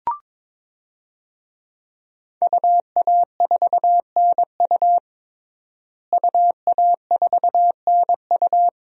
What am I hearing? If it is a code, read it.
Morse code, heard as a keyed tone, about 21 wpm, 700 Hz. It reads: UA4NU UA4NU
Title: DXCC-100 Call Signs - 5 Characters - Repeated 3x - 22wpm